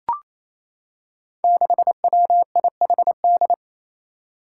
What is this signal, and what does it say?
Morse code, heard as a keyed tone, about 28 wpm, 700 Hz. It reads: 6WIHD
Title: Five Challenging Letter-Number Combinations 28wpm